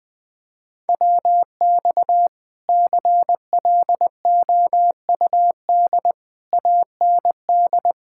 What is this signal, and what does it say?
Morse code, heard as a keyed tone, about 21 wpm, 700 Hz. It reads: WX CLOUD AND
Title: Speed-Racing Common QSO Phrases - 20wpm